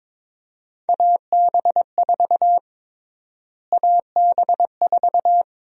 Morse code, heard as a keyed tone, about 22 wpm, 700 Hz. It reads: AB4 AB4
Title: USA Call Sign Prefixes - Repeated 2x - 22wpm